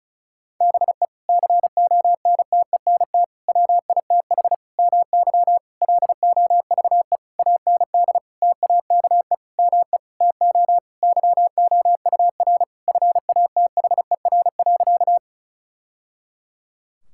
Morse code, heard as a keyed tone, about 36 wpm, 700 Hz. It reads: BE CONTENT WITH MY LOVE AND TAKE ME TO YOUR FATHER.
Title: Sentences from Top 1000 Words - Review Part 6 - 35wpm